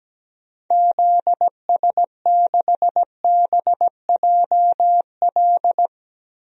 Morse code, heard as a keyed tone, about 17 wpm, 700 Hz. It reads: ZS6BJL
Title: DXCC-100 Call Signs - 6 Characters - 17wpm